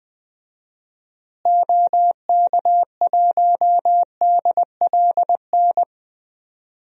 Morse code, heard as a keyed tone, about 20 wpm, 700 Hz. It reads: OK1DLN
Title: DXCC-100 Call Signs - 6 Characters - Repeated 3x - 20wpm